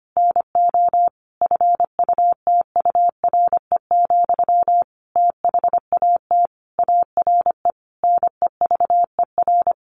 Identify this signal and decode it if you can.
Morse code, heard as a keyed tone, about 25 wpm, 700 Hz. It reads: NO FUTURE, THAT ARE NEVER
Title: A Sound Of Thunder 25wpm